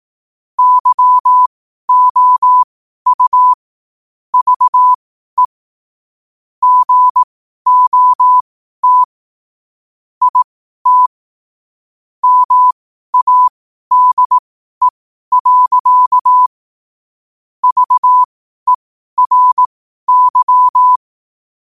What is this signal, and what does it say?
Morse code, heard as a keyed tone, about 18 wpm, spaced as for 12 wpm, 1000 Hz. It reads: YOU VE GOT IT MADE. VERY
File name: qotdfiles / 12wpm